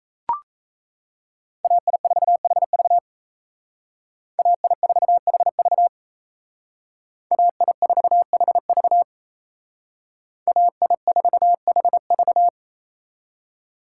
Morse code, heard as a keyed tone, about 32 wpm, 700 Hz. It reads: AI4HV AI4HV AI4HV AI4HV
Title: Speed Racing - USA 2x2 Call Signs 28wpm